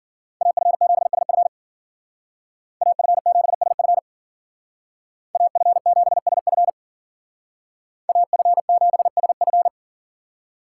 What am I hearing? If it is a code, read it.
Morse code, heard as a keyed tone, about 48 wpm, 700 Hz. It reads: AF7SF AF7SF AF7SF AF7SF
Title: Speed Racing - USA 2x2 Call Signs 40wpm